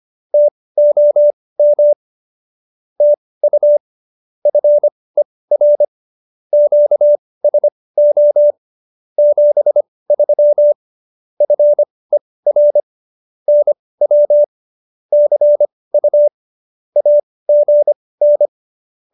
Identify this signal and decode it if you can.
Morse code, heard as a keyed tone, about 25 wpm, spaced as for 18 wpm, 590 Hz. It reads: TOM TU FER QSO 73 FER NW CU AGN
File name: qso201b 18